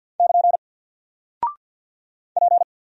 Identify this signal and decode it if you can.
Morse code, heard as a keyed tone, about 50 wpm, 700 Hz. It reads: <KN> P
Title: Single Character (Letter:Number:Punctuation:Prosign) 50wpm